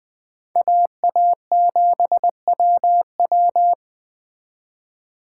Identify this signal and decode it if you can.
Morse code, heard as a keyed tone, about 21 wpm, 700 Hz. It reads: AA7WW